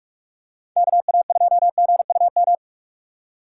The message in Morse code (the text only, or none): KA2GUM